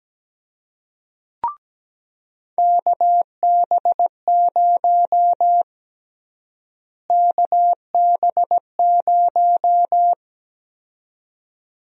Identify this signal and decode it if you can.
Morse code, heard as a keyed tone, about 17 wpm, 700 Hz. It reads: KB0 KB0
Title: USA Call Sign Prefixes - Repeated 2x - 17wpm